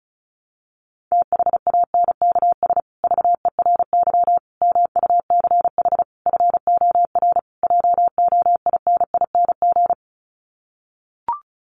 Morse code, heard as a keyed tone, about 34 wpm, 700 Hz. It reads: THANKS VERY MUCH FOR JOINING
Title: Sets of 5 Words Encore2 35wpm